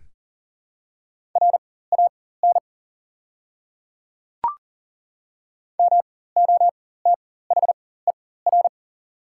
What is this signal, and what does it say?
Morse code, heard as a keyed tone, about 40 wpm, spaced as for 19 wpm, 690 Hz. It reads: RAN MOTHER
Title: 4x Character Spacing - Top 500 Words - 40wpm